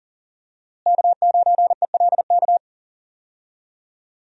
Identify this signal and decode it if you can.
Morse code, heard as a keyed tone, about 40 wpm, 700 Hz. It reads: K9ELK
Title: USA Call Signs - 5 Characters - Repeated 2x - 40wpm